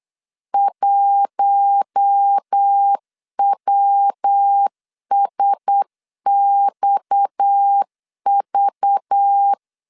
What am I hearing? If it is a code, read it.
Morse code, heard as a keyed tone, about 8 wpm, 800 Hz. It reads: 1WSXV
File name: LEZIONI / 6 PKX0 / 6 PKX 8wpm 3041